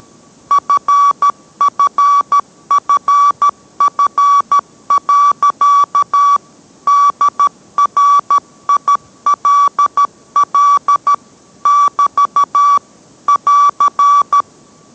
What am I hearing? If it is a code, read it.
Morse code, heard as a keyed tone, about 13 wpm, 1140 Hz. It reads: FFFF. DRILL <BT> <AR>